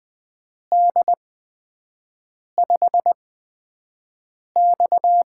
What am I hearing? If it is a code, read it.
Morse code, heard as a keyed tone, about 20 wpm, spaced as for 5 wpm, 700 Hz. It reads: D5X
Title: USA Call Signs - 4 Characters - 8x Character Spacing - 20wpm